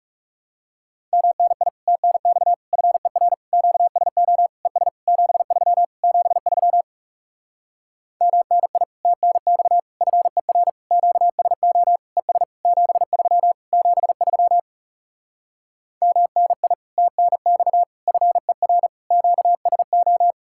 Morse code, heard as a keyed tone, about 40 wpm, 700 Hz. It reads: MNI TNX FER QSO ES 73 73 MNI TNX FER QSO ES 73 73 MNI TNX FER QSO